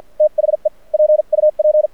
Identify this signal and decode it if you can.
Morse code, heard as a keyed tone, about 57 wpm, spaced as for 45 wpm, 610 Hz. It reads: THE PUP
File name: archived / CRSnet